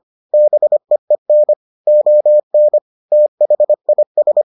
Morse code, heard as a keyed tone, about 25 wpm, 600 Hz. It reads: BEEN ON THIS